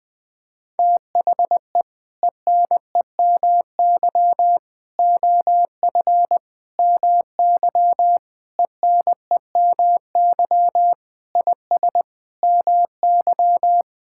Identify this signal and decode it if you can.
Morse code, heard as a keyed tone, about 20 wpm, 700 Hz. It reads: THE ENEMY OF MY ENEMY IS MY
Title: Sentences with 8-Word Phrases - 20wpm